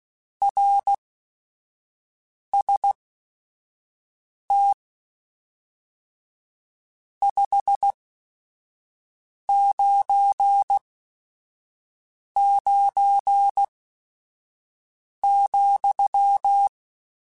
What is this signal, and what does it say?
Morse code, heard as a keyed tone, about 16 wpm, spaced as for 5 wpm, 800 Hz. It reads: RST 599,
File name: Your RSTs @5